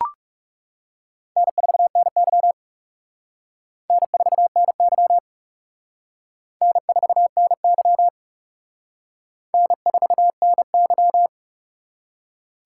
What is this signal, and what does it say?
Morse code, heard as a keyed tone, about 35 wpm, 700 Hz. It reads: N4NY N4NY N4NY N4NY
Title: Speed Racing - USA 1x2 Call Signs 30wpm